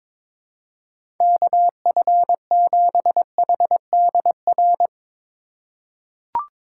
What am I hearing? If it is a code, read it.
Morse code, heard as a keyed tone, about 22 wpm, 700 Hz. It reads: KF7HDR
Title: USA Call Signs - 2x Character Spacing - 22wpm